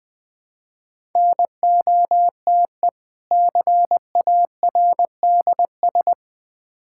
Morse code, heard as a keyed tone, about 20 wpm, 700 Hz. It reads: NOTE CARDS
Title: Difficult Words with Hints - 20wpm